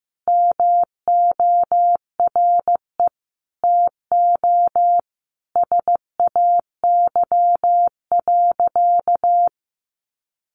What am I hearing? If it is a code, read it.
Morse code, heard as a keyed tone, about 15 wpm, 700 Hz. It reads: MORE TO SAY.
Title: Sentences from Top 600 Words 15wpm